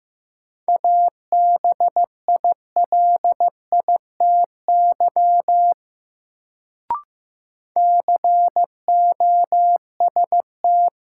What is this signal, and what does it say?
Morse code, heard as a keyed tone, about 15 wpm, 700 Hz. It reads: ABILITY COST
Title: Sets of 3 Words - New Word Drill - 15wpm